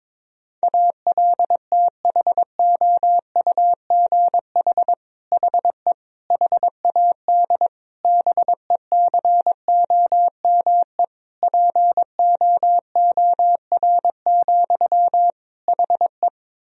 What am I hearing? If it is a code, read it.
Morse code, heard as a keyed tone, about 22 wpm, 700 Hz. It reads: ALTHOUGH HE HAD BECOME POOR, HE
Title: Sentences from Top 1000 Words - Review Part 3 - 22wpm